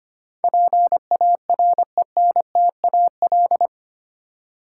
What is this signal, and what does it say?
Morse code, heard as a keyed tone, about 25 wpm, 700 Hz. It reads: PARENTAL
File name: Eight Letter Words - 25wpm